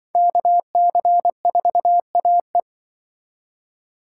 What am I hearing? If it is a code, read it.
Morse code, heard as a keyed tone, about 25 wpm, 700 Hz. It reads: KC4AE
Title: Speed Racing - USA Call Signs 20wpm